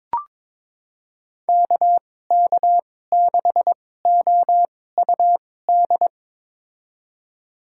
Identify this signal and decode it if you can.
Morse code, heard as a keyed tone, about 21 wpm, spaced as for 16 wpm, 700 Hz. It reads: KK6OUD
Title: USA Call Signs - 6 Characters - 2x Character Spacing - 22wpm